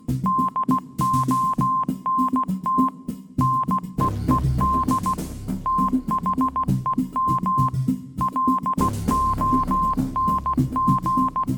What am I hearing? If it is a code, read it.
Morse code, heard as a keyed tone, about 16 wpm, 1050 Hz. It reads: DONT NEED THEM LONG
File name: Kirkoid / Kirkoid - Small Stuff (Morse Mix)